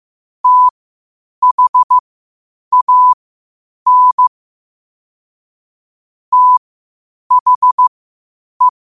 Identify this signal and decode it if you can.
Morse code, heard as a keyed tone, about 15 wpm, spaced as for 8 wpm, 1000 Hz. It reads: THAN THE
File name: qotdfiles / 8wpm